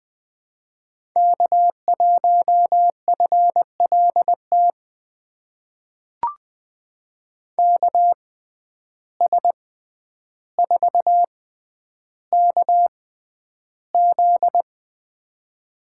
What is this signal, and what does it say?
Morse code, heard as a keyed tone, about 20 wpm, spaced as for 10 wpm, 700 Hz. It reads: K1FLT KS4KZ
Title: USA Call Signs - 5 Characters - 6x Character Spacing - 20wpm